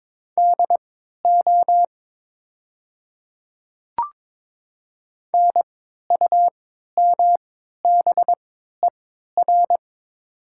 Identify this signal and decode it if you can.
Morse code, heard as a keyed tone, about 22 wpm, spaced as for 12 wpm, 700 Hz. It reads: DO NUMBER
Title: 3x Character Spacing - Top 100 Words - 22wpm